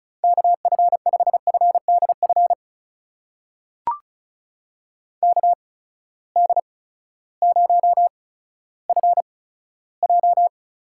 Morse code, heard as a keyed tone, about 36 wpm, spaced as for 16 wpm, 700 Hz. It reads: KF5FDF KD0FJ
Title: USA Call Signs - 6 Characters - 8x Character Spacing - 35wpm